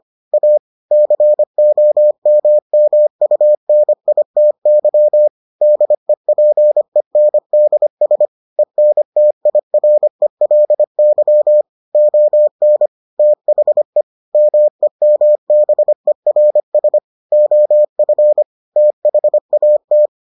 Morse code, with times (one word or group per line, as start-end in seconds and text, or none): 0.3-0.6 A
0.9-5.3 COMMUNITY
5.6-8.3 DEPENDS
8.6-11.6 ENTIRELY
11.9-12.9 ON
13.2-14.0 THE
14.3-17.0 MEMBERS
17.3-18.4 OF
18.8-20.1 THAT